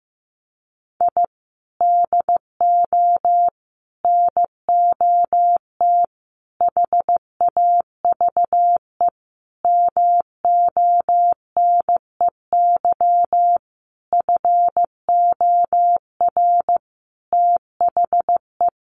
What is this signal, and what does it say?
Morse code, heard as a keyed tone, about 15 wpm, 700 Hz. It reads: I DO NOT HAVE MONEY FOR THE
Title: Sentences from Top 1400 Words 15wpm